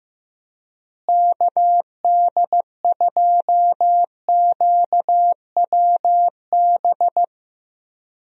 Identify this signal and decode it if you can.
Morse code, heard as a keyed tone, about 15 wpm, 700 Hz. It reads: KD2QWB
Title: USA Call Signs - 6 Characters - Repeated 3x - 15wpm